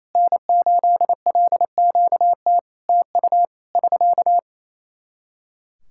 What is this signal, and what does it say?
Morse code, heard as a keyed tone, about 28 wpm, 700 Hz. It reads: N8LQT TU <SK>
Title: Practice QSO - Exchange 4 - Ending - 28wpm